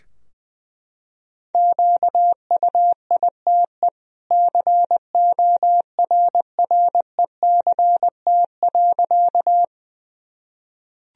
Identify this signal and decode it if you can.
Morse code, heard as a keyed tone, about 20 wpm, 700 Hz. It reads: QUITE CORRECT.
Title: Sentences from Top 1000 Words - Review Part 2 - 20wpm